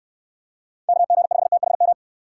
Morse code, heard as a keyed tone, about 67 wpm, 700 Hz. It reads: KG5THM